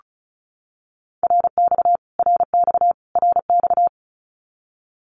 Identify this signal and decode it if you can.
Morse code, heard as a keyed tone, about 34 wpm, 700 Hz. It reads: RX RX RX
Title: Abbreviations Repeated 3x - 35wpm